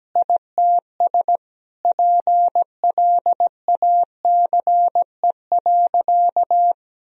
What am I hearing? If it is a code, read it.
Morse code, heard as a keyed tone, about 17 wpm, 700 Hz. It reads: ITS PLACE.
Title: Sentences from Top 1000 Words - Review Part 1 - 17wpm